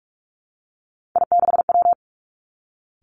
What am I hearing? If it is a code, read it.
Morse code, heard as a keyed tone, about 46 wpm, 700 Hz. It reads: I6W
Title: Three Challenging Letter-Number Combinations 45wpm